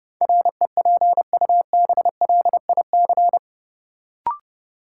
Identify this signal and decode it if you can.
Morse code, heard as a keyed tone, about 32 wpm, 700 Hz. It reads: REPUBLIC
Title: Sentences from Top 1500 Words - New Word Drill Alt - 30wpm